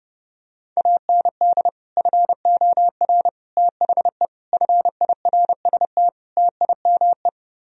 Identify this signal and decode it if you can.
Morse code, heard as a keyed tone, about 29 wpm, 700 Hz. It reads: AND FOR THE FIRST TIME